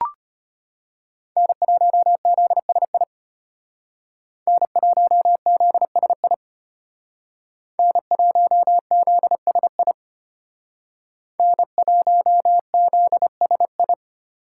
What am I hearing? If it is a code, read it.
Morse code, heard as a keyed tone, about 30 wpm, 700 Hz. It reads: N1ZSI N1ZSI N1ZSI N1ZSI
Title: Speed Racing - USA 1x3 Call Signs 25wpm